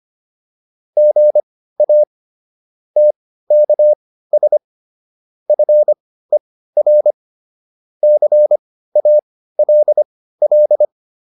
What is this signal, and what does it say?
Morse code, heard as a keyed tone, about 25 wpm, spaced as for 15 wpm, 600 Hz. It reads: GA TKS FER CALL
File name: QSO401 15